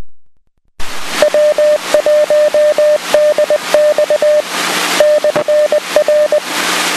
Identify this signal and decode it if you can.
Morse code, heard as a keyed tone, about 20 wpm, 600 Hz, with noise. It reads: W1DX /R